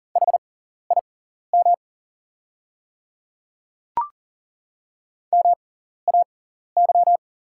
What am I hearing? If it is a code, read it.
Morse code, heard as a keyed tone, about 40 wpm, spaced as for 14 wpm, 690 Hz. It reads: HIM MAY
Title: 6x Character Spacing - Top 100 Words - 40wpm